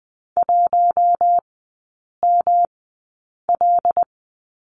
Morse code, heard as a keyed tone, about 20 wpm, spaced as for 8 wpm, 690 Hz. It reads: 1ML
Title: 145 Lesson 24e - Letter y - Three Characters 20wpm